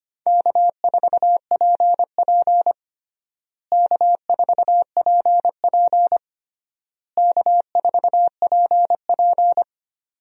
Morse code, heard as a keyed tone, about 25 wpm, 700 Hz. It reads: K4PP K4PP K4PP
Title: USA Call Signs - 4 Characters - Repeated 3x - 25wpm